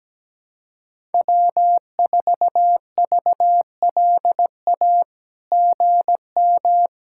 Morse code, heard as a keyed tone, about 17 wpm, 700 Hz. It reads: W4VLA GM
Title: SOTA QSO Practice 17wpm